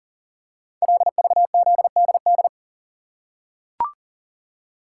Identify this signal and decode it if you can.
Morse code, heard as a keyed tone, about 42 wpm, 690 Hz. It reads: LVZDD
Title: Code Groups - 5 Letters - 40wpm